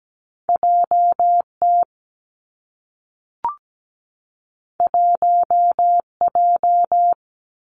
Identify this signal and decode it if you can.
Morse code, heard as a keyed tone, about 17 wpm, 700 Hz. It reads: J T 1 J
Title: Two Challenging Letter-Number Combinations 17wpm